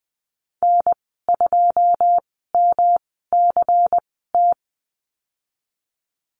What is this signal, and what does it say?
Morse code, heard as a keyed tone, about 20 wpm, spaced as for 15 wpm, 700 Hz. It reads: N2MCT
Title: 147 Lesson 24g - Letter y - Call Signs 20wpm